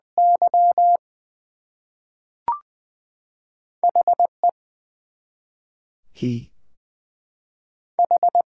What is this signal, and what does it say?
Morse code, heard as a keyed tone, about 20 wpm, 700 Hz. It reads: Y HE H
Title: Two Letter Words - 20wpm